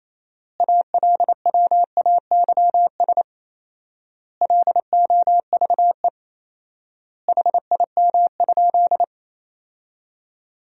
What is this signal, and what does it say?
Morse code, heard as a keyed tone, about 28 wpm, 700 Hz. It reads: ALWAYS LOVE HIM?
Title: 4x Word-Spacing - Sentences from Top 400 Words - 28wpm